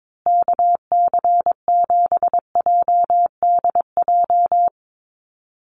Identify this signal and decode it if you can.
Morse code, heard as a keyed tone, about 22 wpm, 700 Hz. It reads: KC7JDJ
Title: USA Call Signs Repeated 2x 22wpm